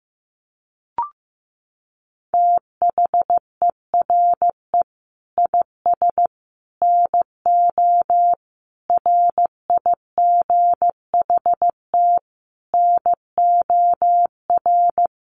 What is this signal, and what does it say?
Morse code, heard as a keyed tone, about 15 wpm, 700 Hz. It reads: THERE IS NO RIGHT NOR
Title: Sentences from Top 1500 Words 15wpm